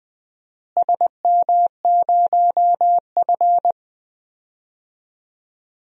Morse code, heard as a keyed tone, about 20 wpm, 700 Hz. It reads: SM0F
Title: DXCC-100 Call Signs - 4 Characters - Repeated 2x - 20wpm